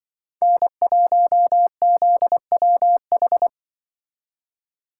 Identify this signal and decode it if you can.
Morse code, heard as a keyed tone, about 24 wpm, 700 Hz. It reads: N1ZWH
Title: Speed Racing - USA Call Signs 20wpm